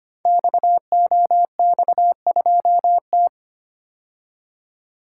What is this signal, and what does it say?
Morse code, heard as a keyed tone, about 25 wpm, 700 Hz. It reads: XOX2T
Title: Code Groups - 5 Letter:Numbers - 25wpm